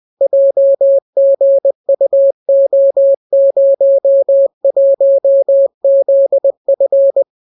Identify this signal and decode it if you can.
Morse code, heard as a keyed tone, about 20 wpm, 550 Hz. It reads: JGUO01ZF